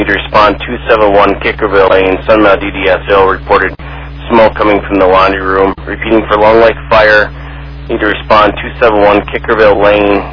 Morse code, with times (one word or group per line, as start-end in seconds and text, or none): none